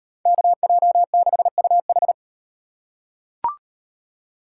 Morse code, heard as a keyed tone, about 38 wpm, 700 Hz. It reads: KJ6UH